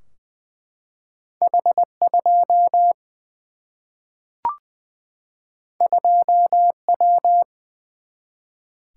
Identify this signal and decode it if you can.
Morse code, heard as a keyed tone, about 20 wpm, 700 Hz. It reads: H 2 2 W
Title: Single-Letter Single-Number 20wpm